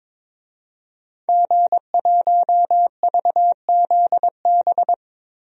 Morse code, heard as a keyed tone, about 22 wpm, 700 Hz. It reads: G1VZB
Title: DXCC-100 Call Signs - 5 Characters - Repeated 3x - 22wpm